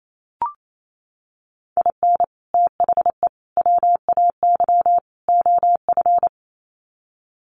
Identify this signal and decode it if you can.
Morse code, heard as a keyed tone, about 28 wpm, 700 Hz. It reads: IN THE WAY OF
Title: Sets of 4 Words 28wpm